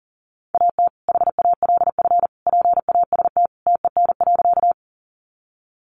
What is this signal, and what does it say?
Morse code, heard as a keyed tone, about 40 wpm, 700 Hz. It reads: AT HALF PAST TEN.
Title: Sentences from Top 600 Words 40wpm